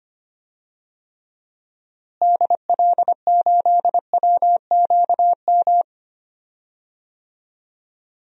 Morse code, heard as a keyed tone, about 25 wpm, 700 Hz. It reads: DL8WQM
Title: DXCC-100 Call Signs - 6 Characters - Repeated 2x - 25wpm